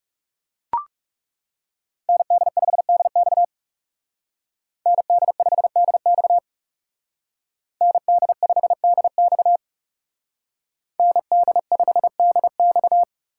Morse code, heard as a keyed tone, about 36 wpm, 700 Hz. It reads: ND5DX ND5DX ND5DX ND5DX
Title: Speed Racing - USA 2x2 Call Signs 30wpm